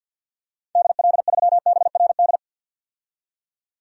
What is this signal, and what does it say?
Morse code, heard as a keyed tone, about 46 wpm, 700 Hz. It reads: DL3BRD
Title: DXCC-100 Call Signs - 6 Characters - 50wpm